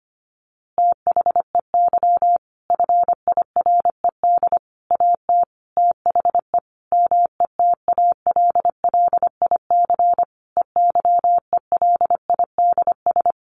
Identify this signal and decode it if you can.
Morse code, heard as a keyed tone, about 25 wpm, 700 Hz. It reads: THEY FIRED AT THE METALLIC EYELIDS
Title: A Sound Of Thunder 25wpm